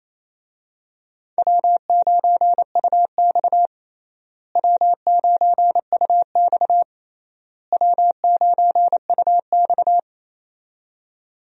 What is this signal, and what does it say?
Morse code, heard as a keyed tone, about 28 wpm, 700 Hz. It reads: W9UX W9UX W9UX
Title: USA Call Signs - 4 Characters - Repeated 3x - 28wpm